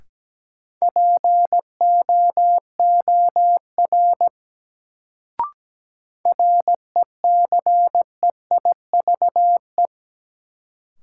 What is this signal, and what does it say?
Morse code, heard as a keyed tone, about 17 wpm, 700 Hz. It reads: POOR RECEIVE
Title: Sentences from Top 1000 Words - New Word Drill - 17wpm